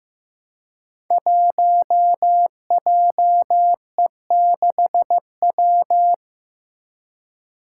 Morse code, heard as a keyed tone, about 15 wpm, 700 Hz. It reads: 1JE6W
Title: Five Challenging Letter-Number Combinations 15wpm